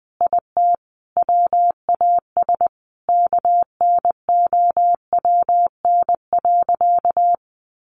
Morse code, heard as a keyed tone, about 19 wpm, 700 Hz. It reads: IT WAS KNOWN.
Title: Sentences from Top 1400 Words 20wpm